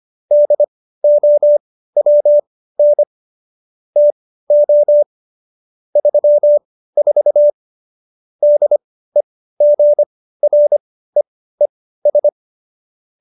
Morse code, heard as a keyed tone, about 25 wpm, spaced as for 15 wpm, 590 Hz. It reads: DOWN TO 34 DEGREES